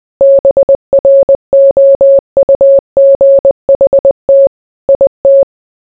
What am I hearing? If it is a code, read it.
Morse code, heard as a keyed tone, about 20 wpm, 550 Hz. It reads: BROUGHT IT